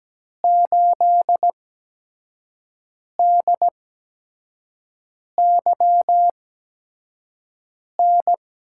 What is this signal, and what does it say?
Morse code, heard as a keyed tone, about 17 wpm, spaced as for 5 wpm, 700 Hz. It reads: 8DYN
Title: USA Call Signs - 6 Characters - 8x Character Spacing - 17wpm